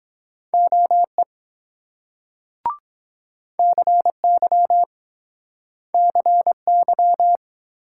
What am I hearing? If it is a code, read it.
Morse code, heard as a keyed tone, about 23 wpm, 700 Hz. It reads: OE CY CY